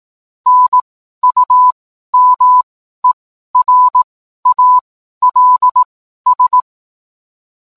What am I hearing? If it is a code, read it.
Morse code, heard as a keyed tone, about 18 wpm, spaced as for 12 wpm, 1000 Hz. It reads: NUMERALS